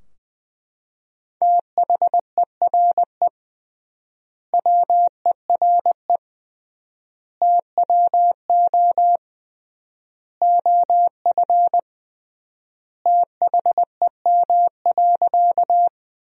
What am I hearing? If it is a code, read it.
Morse code, heard as a keyed tone, about 20 wpm, 700 Hz. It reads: THERE WERE TWO OF THEM.
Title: 3x Word-Spacing - Sentences from Top 100 Words - 20wpm